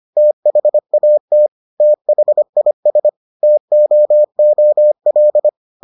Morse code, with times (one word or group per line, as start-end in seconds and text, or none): 0.2-1.5 THAT
1.8-3.1 THIS
3.4-5.5 TOOL